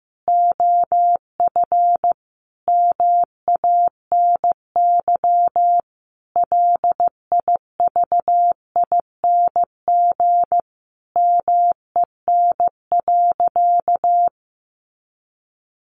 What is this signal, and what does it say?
Morse code, heard as a keyed tone, about 15 wpm, 700 Hz. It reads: OF MANY LIVING MEN.
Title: Sentences from Top 1300 Words 15wpm